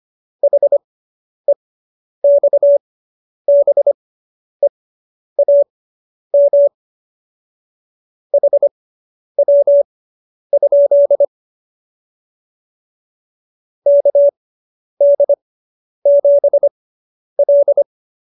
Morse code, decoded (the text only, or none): HEXBEAM HW? KD7L